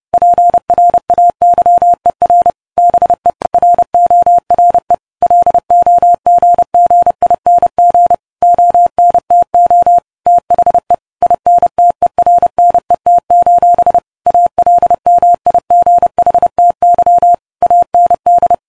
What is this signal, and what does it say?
Morse code, heard as a keyed tone, about 30 wpm, 700 Hz. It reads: PRAYER BE RORE LOGGING ONTO THE INTERNET: ALMIGHTY AND